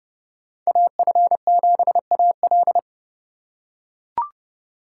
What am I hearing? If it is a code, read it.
Morse code, heard as a keyed tone, about 28 wpm, 700 Hz. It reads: AF7AL